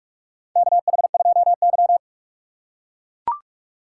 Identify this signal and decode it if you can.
Morse code, heard as a keyed tone, about 41 wpm, 700 Hz. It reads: KH2Y
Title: USA Call Signs - 4 Characters - 2x Character Spacing - 45wpm